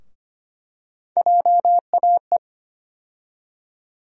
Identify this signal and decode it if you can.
Morse code, heard as a keyed tone, about 25 wpm, 700 Hz. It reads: JAE
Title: USA Call Sign Postfixes - 3 Letters 25wpm